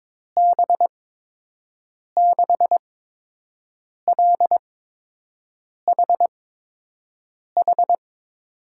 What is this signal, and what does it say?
Morse code, heard as a keyed tone, about 22 wpm, spaced as for 6 wpm, 700 Hz. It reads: B6LHH
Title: USA Call Signs - 6 Characters - 8x Character Spacing - 22wpm